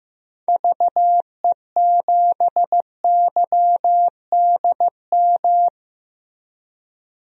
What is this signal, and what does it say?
Morse code, heard as a keyed tone, about 15 wpm, 700 Hz. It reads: VE7YDM
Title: DXCC-100 Call Signs - 6 Characters - Repeated 2x - 15wpm